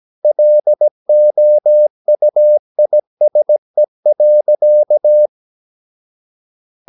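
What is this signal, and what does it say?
Morse code, heard as a keyed tone, about 17 wpm, 600 Hz. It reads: LOUISE.
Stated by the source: Slow (QRS) Morse practice at 17 wpm, from CW med Gnister.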